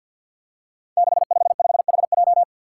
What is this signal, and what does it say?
Morse code, heard as a keyed tone, about 52 wpm, 700 Hz. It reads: 655HJ
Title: Five Challenging Letter-Number Combinations 50wpm